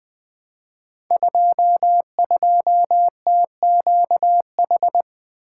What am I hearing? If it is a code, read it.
Morse code, heard as a keyed tone, about 20 wpm, 700 Hz. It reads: 22TQH
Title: Code Groups - 5 Letter:Numbers - 20wpm